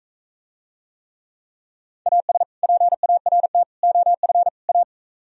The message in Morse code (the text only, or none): AS PART OF A